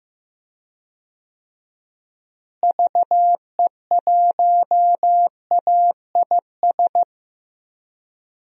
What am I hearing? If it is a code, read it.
Morse code, heard as a keyed tone, about 15 wpm, 700 Hz. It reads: VE1AIS
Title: DXCC-100 Call Signs - 6 Characters - Repeated 2x - 15wpm